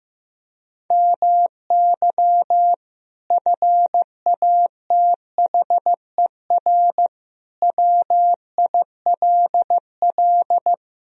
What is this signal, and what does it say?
Morse code, heard as a keyed tone, about 16 wpm, 700 Hz. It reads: MY FATHER WILL
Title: Sentences from Top 1000 Words - Review Part 4 - 15wpm